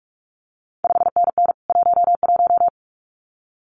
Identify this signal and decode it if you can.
Morse code, heard as a keyed tone, about 46 wpm, 700 Hz. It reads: 5NN 11
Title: CQ WW DX Contest Exchange Encore - 45wpm